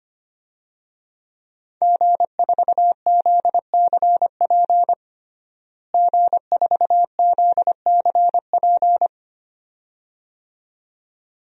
Morse code, heard as a keyed tone, about 25 wpm, 700 Hz. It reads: G4ZCP G4ZCP
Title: DXCC-100 Call Signs - 5 Characters - Repeated 2x - 25wpm